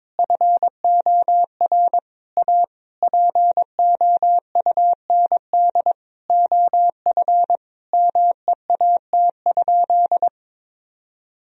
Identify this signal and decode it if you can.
Morse code, heard as a keyed tone, about 22 wpm, 700 Hz. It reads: FOR A POUND OF MEAT?
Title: Sentences from Top 1000 Words - Review Part 3 - 22wpm